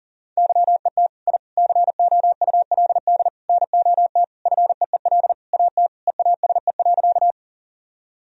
Morse code, heard as a keyed tone, about 40 wpm, 700 Hz. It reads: YET I COULD NOT FEEL AT EASE.
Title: Sentences from Top 1000 Words - Review Part 1 - 40wpm